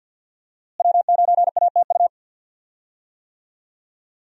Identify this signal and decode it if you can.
Morse code, heard as a keyed tone, about 47 wpm, 700 Hz. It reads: W9ATU
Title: USA Call Signs - 5 Characters - Repeated 3x - 50wpm